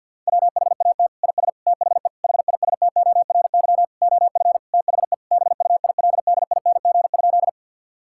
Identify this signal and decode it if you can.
Morse code, heard as a keyed tone, about 50 wpm, 700 Hz. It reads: WHAT IS THE HISTORY OF THE BUILDING?
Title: Sentences from Top 1000 Words - Review Part 5 - 50wpm